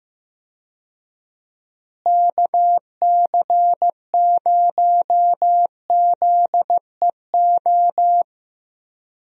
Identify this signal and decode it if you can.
Morse code, heard as a keyed tone, about 16 wpm, 700 Hz. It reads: KC0ZEO
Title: USA Call Signs - 6 Characters - Repeated 2x - 15wpm